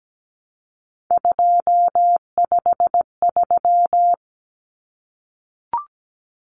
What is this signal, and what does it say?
Morse code, heard as a keyed tone, about 17 wpm, 700 Hz. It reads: 253
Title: Numbers - 3-Digit 17wpm